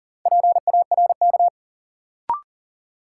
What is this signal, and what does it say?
Morse code, heard as a keyed tone, about 37 wpm, 700 Hz. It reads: PARK
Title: Sentences from Top 1300 Words - New Word Drill - 40wpm